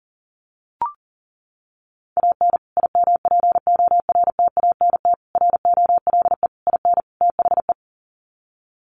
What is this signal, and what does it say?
Morse code, heard as a keyed tone, about 37 wpm, 700 Hz. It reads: AN IMPORTANT ROLE IN THE
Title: Sets of 5 Words Encore2 40wpm